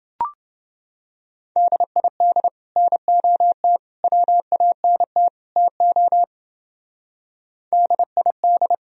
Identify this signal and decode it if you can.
Morse code, heard as a keyed tone, about 28 wpm, 700 Hz. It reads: DID NOT WANT TO DID